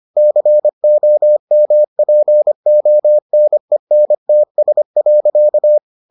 Morse code, heard as a keyed tone, about 25 wpm, 600 Hz. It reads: COMPONENTS.